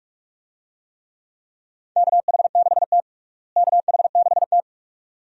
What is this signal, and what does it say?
Morse code, heard as a keyed tone, about 41 wpm, 700 Hz. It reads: KH6T KH6T
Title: DXCC-100 Call Signs - 4 Characters - Repeated 2x - 45wpm